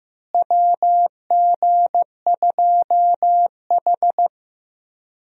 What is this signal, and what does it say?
Morse code, heard as a keyed tone, about 15 wpm, 700 Hz. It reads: WG2H
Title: USA Call Signs - 4 Characters - Repeated 2x - 15wpm